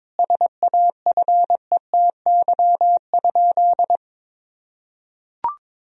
Morse code, heard as a keyed tone, about 21 wpm, 700 Hz. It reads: SAFETY?
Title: Sentences from Top 1000 Words - Review Part 4 - 22wpm